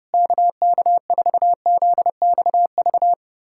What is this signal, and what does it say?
Morse code, heard as a keyed tone, about 30 wpm, 700 Hz. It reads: KK4ZXV
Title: USA Callsign DE Callsign 30wpm